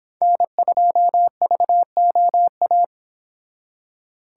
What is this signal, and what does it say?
Morse code, heard as a keyed tone, about 30 wpm, 700 Hz. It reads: N2VOA